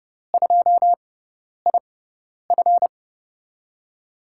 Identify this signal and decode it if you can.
Morse code, heard as a keyed tone, about 30 wpm, spaced as for 10 wpm, 700 Hz. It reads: 2IF
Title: USA Call Signs - 5 Characters - 6x Character Spacing - 30wpm